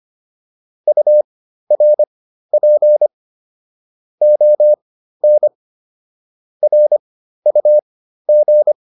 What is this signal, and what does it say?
Morse code, heard as a keyed tone, about 25 wpm, spaced as for 13 wpm, 600 Hz. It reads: URP ON RUG